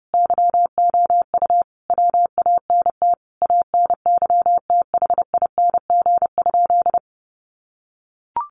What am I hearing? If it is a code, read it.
Morse code, heard as a keyed tone, about 30 wpm, 700 Hz. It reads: YOU WANT ANYTHING?
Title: Sentences from Top 1000 Words 30wpm